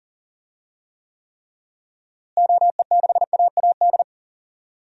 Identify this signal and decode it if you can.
Morse code, heard as a keyed tone, about 40 wpm, 700 Hz. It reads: OE6AAD
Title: DXCC-100 Call Signs - 6 Characters - Repeated 3x - 40wpm